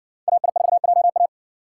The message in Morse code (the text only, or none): AI4JA